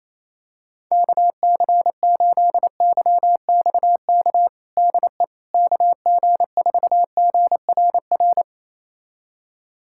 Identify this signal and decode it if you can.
Morse code, heard as a keyed tone, about 28 wpm, 700 Hz. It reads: KC8YXK DE KG4GRR
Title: USA Callsign DE Callsign - 6 Characters - 28wpm